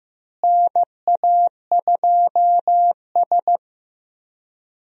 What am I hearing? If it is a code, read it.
Morse code, heard as a keyed tone, about 15 wpm, 700 Hz. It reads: NA2S
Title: USA Call Signs - 4 Characters - Repeated 3x - 15wpm